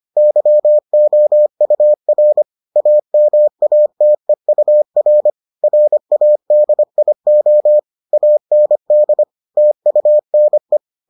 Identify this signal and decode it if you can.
Morse code, heard as a keyed tone, about 25 wpm, 600 Hz. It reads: YOUR AMATEUR RADIO AND TUNE